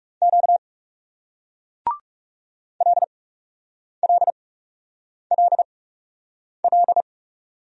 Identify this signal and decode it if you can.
Morse code, heard as a keyed tone, about 38 wpm, spaced as for 8 wpm, 700 Hz. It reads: Q LLLL